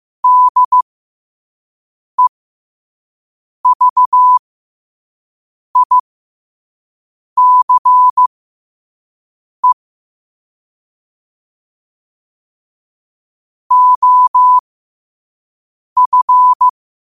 Morse code, heard as a keyed tone, about 15 wpm, spaced as for 5 wpm, 1000 Hz. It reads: DEVICE OF
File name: 2026-02-15-5wpm